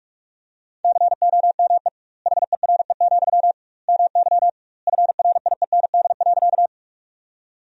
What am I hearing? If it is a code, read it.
Morse code, heard as a keyed tone, about 46 wpm, 700 Hz. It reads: COME HERE, MY FRIEND.
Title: 2x Word-Spacing - Sentences from Top 300 Words - 45wpm